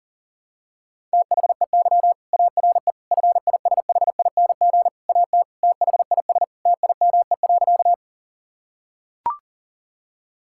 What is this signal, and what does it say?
Morse code, heard as a keyed tone, about 40 wpm, 700 Hz. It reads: THEY ARE FISHING AT THIS TIME.
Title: Sentences from Top 1000 Words - Review Part 5 - 40wpm